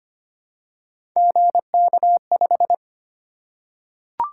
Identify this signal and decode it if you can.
Morse code, heard as a keyed tone, about 25 wpm, 700 Hz. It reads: GK5
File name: Three Letter-Number Combinations - 25wpm